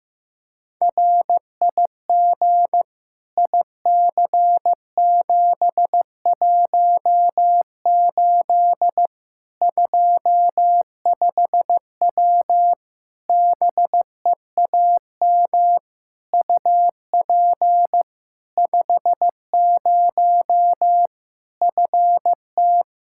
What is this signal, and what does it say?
Morse code, heard as a keyed tone, about 15 wpm, 700 Hz. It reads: RIG IC718 25W BEAM UP 50 FT
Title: K1USN SST QSO Practice 15wpm